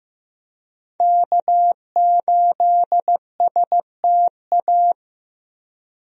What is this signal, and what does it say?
Morse code, heard as a keyed tone, about 15 wpm, 700 Hz. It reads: K8STA
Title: USA Call Signs - 5 Characters - Repeated 2x - 15wpm